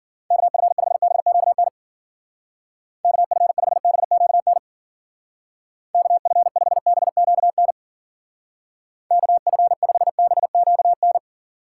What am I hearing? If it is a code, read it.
Morse code, heard as a keyed tone, about 50 wpm, 700 Hz. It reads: KF5BQN KF5BQN KF5BQN KF5BQN